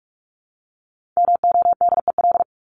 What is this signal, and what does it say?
Morse code, heard as a keyed tone, about 44 wpm, 700 Hz. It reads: MODEL
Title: Top 501-750 Words 45wpm